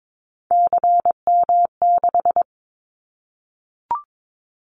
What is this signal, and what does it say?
Morse code, heard as a keyed tone, about 22 wpm, 700 Hz. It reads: CM6
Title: Callsign Prefixes for Top 100 DXCC - 22wpm